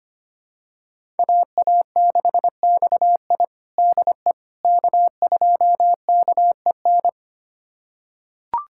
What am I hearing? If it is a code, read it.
Morse code, heard as a keyed tone, about 25 wpm, 700 Hz. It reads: AA6XI DE K2KEN
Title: USA Callsign DE Callsign - 5 Characters - 25wpm